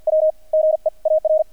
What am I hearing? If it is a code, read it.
Morse code, heard as a keyed tone, about 48 wpm, spaced as for 40 wpm, 630 Hz. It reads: W GEAR
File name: archived / CRSnet